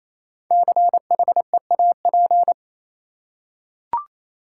Morse code, heard as a keyed tone, about 28 wpm, 700 Hz. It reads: CHEAP
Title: Five Letter Words - 28wpm